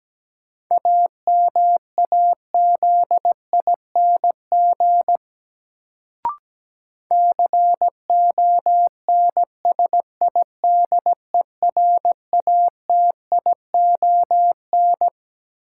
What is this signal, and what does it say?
Morse code, heard as a keyed tone, about 17 wpm, 700 Hz. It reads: AMAZING CONSIDERATION
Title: Sentences from Top 1500 Words - New Word Drill - 17wpm